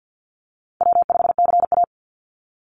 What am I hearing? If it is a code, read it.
Morse code, heard as a keyed tone, about 50 wpm, 700 Hz. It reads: W5CA
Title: USA Call Signs 50wpm